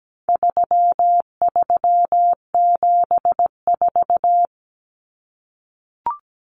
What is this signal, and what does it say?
Morse code, heard as a keyed tone, about 17 wpm, 700 Hz. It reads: 3374
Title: Numbers - 4-Digit 17wpm